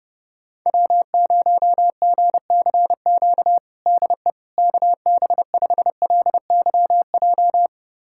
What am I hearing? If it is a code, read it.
Morse code, heard as a keyed tone, about 32 wpm, 700 Hz. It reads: W0GCQ DE KB5LYJ
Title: USA Callsign DE Callsign 30wpm